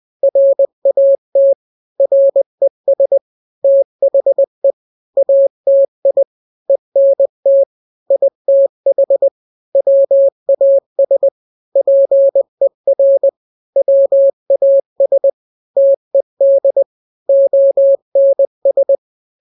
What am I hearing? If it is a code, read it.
Morse code, heard as a keyed tone, about 20 wpm, 550 Hz. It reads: RAT RES THE ATI ENT ITH WAS PER WAS TED ONS
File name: n-grams / 3-grams 20 18 WPM